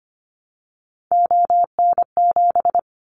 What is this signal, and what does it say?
Morse code, heard as a keyed tone, about 25 wpm, 700 Hz. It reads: ON7
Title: Callsign Prefixes for Top 50 DXCC - 25wpm